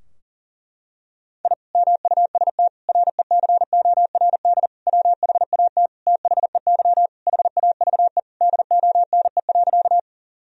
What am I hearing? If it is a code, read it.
Morse code, heard as a keyed tone, about 40 wpm, 700 Hz. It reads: I MUST RECORD WHAT THEY HAVE DONE.
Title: Sentences from Top 1000 Words - Review Part 3 - 40wpm